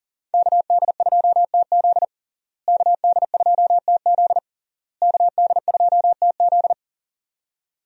Morse code, heard as a keyed tone, about 40 wpm, 700 Hz. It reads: KD2TZ KD2TZ KD2TZ
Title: USA Call Signs - 5 Characters - Repeated 3x - 40wpm